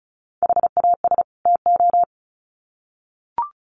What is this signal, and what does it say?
Morse code, heard as a keyed tone, about 34 wpm, 700 Hz. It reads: HAS TO
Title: Sets of 2 Words 35wpm